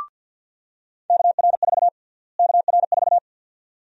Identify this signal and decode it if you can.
Morse code, heard as a keyed tone, about 52 wpm, 700 Hz. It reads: KR4 KR4
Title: USA Call Sign Prefixes - Repeated 2x - 50wpm